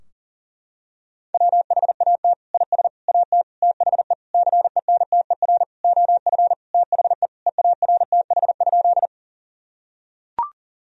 Morse code, heard as a keyed tone, about 37 wpm, 700 Hz. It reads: WHAT IS AT THE CENTER OF THE EARTH?
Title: Sentences from Top 1000 Words - Review Part 2 - 40wpm